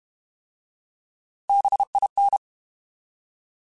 Morse code, heard as a keyed tone, about 32 wpm, 800 Hz. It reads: BIN
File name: Alphabetical / Alphabetical mp3s / Bs @30